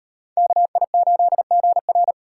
Morse code, heard as a keyed tone, about 37 wpm, 700 Hz. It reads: KI8GR